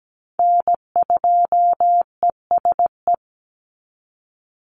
Morse code, heard as a keyed tone, about 17 wpm, 700 Hz. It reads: N2ESE
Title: NA SOTA Chaser Callsigns 17wpm